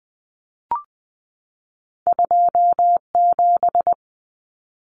Morse code, heard as a keyed tone, about 20 wpm, 700 Hz. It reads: 27
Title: Common Cut Numbers - 2-Digit 20wpm